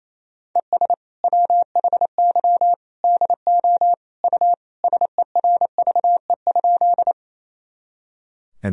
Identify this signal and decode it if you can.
Morse code, heard as a keyed tone, about 28 wpm, 700 Hz. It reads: ES WHY DO U SERVE?
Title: Sentences from Top 400 Words and Abbreviations 28wpm